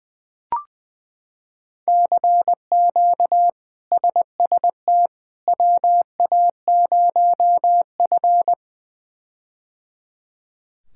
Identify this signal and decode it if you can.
Morse code, heard as a keyed tone, about 20 wpm, 700 Hz. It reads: CQ SST WA0F
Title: K1USN SST QSO Practice - 20wpm